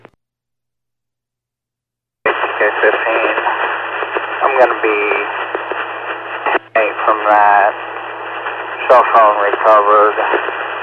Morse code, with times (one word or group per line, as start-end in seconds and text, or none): none